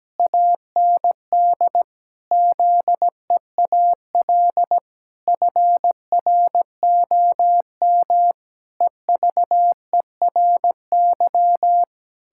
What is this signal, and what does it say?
Morse code, heard as a keyed tone, about 17 wpm, 700 Hz. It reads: AND ZEAL FROM EVERY
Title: Pangram Challenge v2 - 17wpm